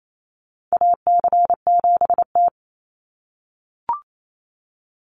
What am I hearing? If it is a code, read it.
Morse code, heard as a keyed tone, about 28 wpm, 700 Hz. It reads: AC7T
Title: Call Signs - 28wpm